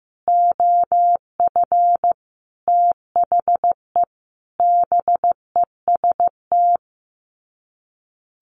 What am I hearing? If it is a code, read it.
Morse code, heard as a keyed tone, about 15 wpm, 700 Hz. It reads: OF THE BEST
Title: Sets of 5 Words 15wpm